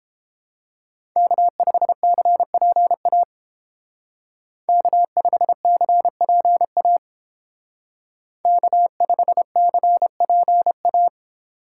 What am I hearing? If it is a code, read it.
Morse code, heard as a keyed tone, about 28 wpm, 700 Hz. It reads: K5CPA K5CPA K5CPA